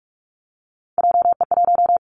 Morse code, heard as a keyed tone, about 46 wpm, 700 Hz. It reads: JE1
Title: Callsign Prefixes for Top 25 DXCC - 45wpm